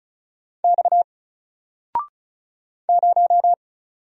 Morse code, heard as a keyed tone, about 37 wpm, 700 Hz. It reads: X 0
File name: Single-Letter-Number-35wpm